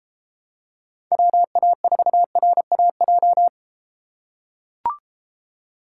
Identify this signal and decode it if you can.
Morse code, heard as a keyed tone, about 32 wpm, 700 Hz. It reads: WA4RAJ